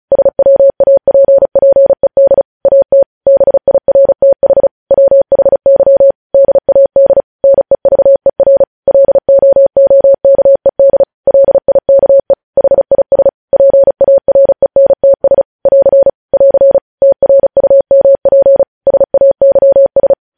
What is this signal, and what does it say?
Morse code, heard as a keyed tone, about 36 wpm, 550 Hz. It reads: SWAPPED AT BIRTH WHY DAD NEVER LOOKED LIKE HIS PARENTS <AR> <AR> TRUMP SAYS
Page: news headlines in morse code at 35 wpm